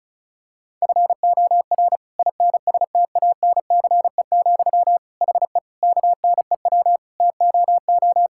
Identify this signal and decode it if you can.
Morse code, heard as a keyed tone, about 36 wpm, 700 Hz. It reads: FOR INSTANCE, HE KNEW TOO
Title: Sentences from Top 1000 Words - Review Part 5 - 35wpm